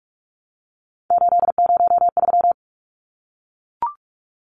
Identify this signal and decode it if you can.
Morse code, heard as a keyed tone, about 44 wpm, 700 Hz. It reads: 803
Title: Numbers - 3-Digit 45wpm